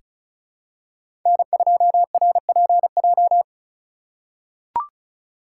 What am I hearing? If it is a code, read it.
Morse code, heard as a keyed tone, about 34 wpm, 700 Hz. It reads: N2RPJ
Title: USA Call Signs - 5 Characters - 4x Character Spacing - 35wpm